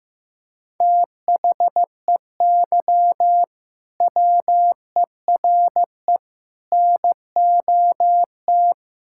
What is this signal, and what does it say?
Morse code, heard as a keyed tone, about 15 wpm, 700 Hz. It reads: THEY WERE NOT
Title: Sentences from Top 1000 Words - Review Part 6 - 15wpm